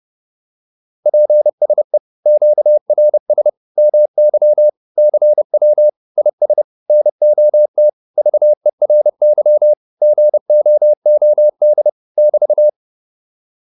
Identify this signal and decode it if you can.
Morse code, heard as a keyed tone, about 30 wpm, 600 Hz. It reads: PSE QRS MY CW IS NOT VERY GOOD <BT>